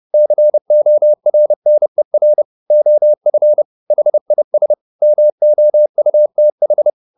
Morse code, heard as a keyed tone, about 30 wpm, 600 Hz. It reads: CORNER OF HIS MOUTH